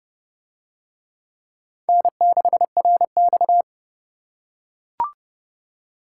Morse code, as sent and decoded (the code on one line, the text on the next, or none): -. -.... .-. -..-
N6RX